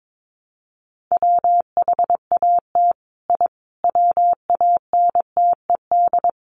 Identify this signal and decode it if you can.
Morse code, heard as a keyed tone, about 22 wpm, 700 Hz. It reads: WHAT I WANTED
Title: Sets of 5 Words Encore2 22wpm